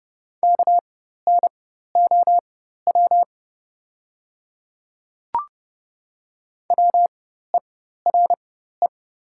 Morse code, heard as a keyed tone, about 30 wpm, spaced as for 14 wpm, 700 Hz. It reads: KNOW WERE
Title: 4x Character Spacing - Top 500 Words - 30wpm